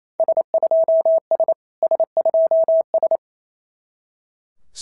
Morse code, heard as a keyed tone, about 29 wpm, 650 Hz. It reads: S2S S2S
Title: SOTA QSO Practice 28wpm